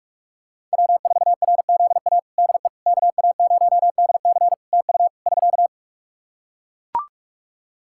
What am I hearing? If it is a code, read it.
Morse code, heard as a keyed tone, about 44 wpm, 700 Hz. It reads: W4RZA DE KA0DC TU <SK>
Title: Practice QSO - Exchange 4 - Ending - 45wpm